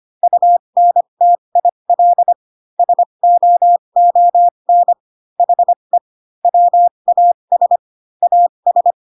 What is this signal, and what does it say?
Morse code, heard as a keyed tone, about 25 wpm, spaced as for 22 wpm, 700 Hz. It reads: UNTIL SOON HE WAS AS